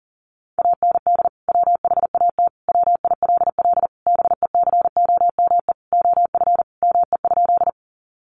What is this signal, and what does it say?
Morse code, heard as a keyed tone, about 40 wpm, 700 Hz. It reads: AND WHAT WILL BECOME OF ME?
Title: Sentences from Top 600 Words 40wpm